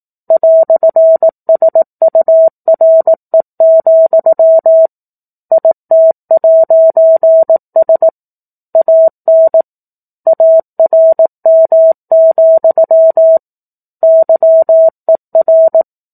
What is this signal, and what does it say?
Morse code, heard as a keyed tone, about 18 wpm, 650 Hz. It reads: "SURE, IT'S AN ARM, YER
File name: 04-Chapter-04 18 WPM